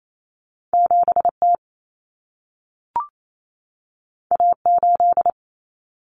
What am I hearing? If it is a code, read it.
Morse code, heard as a keyed tone, about 28 wpm, 700 Hz. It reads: 7T A8
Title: Common Cut Numbers - 2-Digit 28wpm